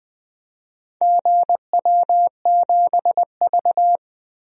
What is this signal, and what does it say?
Morse code, heard as a keyed tone, about 20 wpm, 700 Hz. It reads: GW7V
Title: DXCC-100 Call Signs - 4 Characters - 20wpm